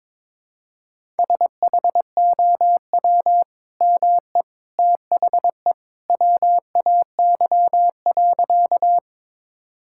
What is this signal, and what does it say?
Morse code, heard as a keyed tone, about 22 wpm, 700 Hz. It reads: SHOW ME THE WAY.
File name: practice / 22wpm / Sentences with 2-Word Phrases - 22wpm